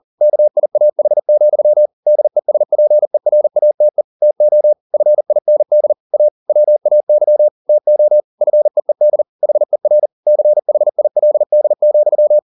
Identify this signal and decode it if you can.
Morse code, heard as a keyed tone, about 40 wpm, 600 Hz. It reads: KIAH, DESPERATE TO FIND A WAY TO FEED HER CHILD,